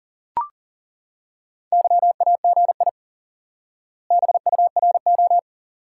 Morse code, heard as a keyed tone, about 40 wpm, 700 Hz. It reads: YAGI BURO
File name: IQE Course 3 - Lesson 89 - YAGI - Two Elements - 40wpm